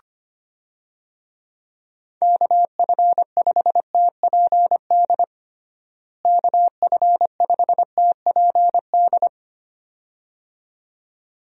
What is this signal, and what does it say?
Morse code, heard as a keyed tone, about 25 wpm, 700 Hz. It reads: KF5TPD KF5TPD
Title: USA Call Signs - 6 Characters - Repeated 2x - 25wpm